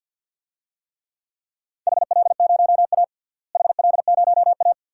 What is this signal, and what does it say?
Morse code, heard as a keyed tone, about 49 wpm, 700 Hz. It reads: HL0A HL0A
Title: DXCC-100 Call Signs - 4 Characters - Repeated 2x - 50wpm